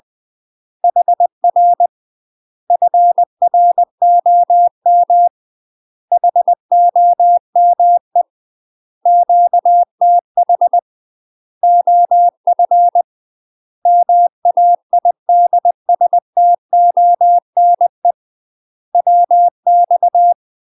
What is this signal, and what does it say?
Morse code, heard as a keyed tone, about 20 wpm, 700 Hz. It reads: HR FROM HOME QTH OF MAIDSTONE WX
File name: UK MASTER QSOs TRAINING / 20WPM / UK MASTER QSOs 0062